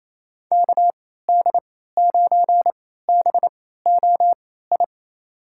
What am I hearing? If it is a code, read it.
Morse code, heard as a keyed tone, about 28 wpm, spaced as for 16 wpm, 700 Hz. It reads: KD9BOI
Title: USA Call Signs - 6 Characters - 3x Character Spacing - 28wpm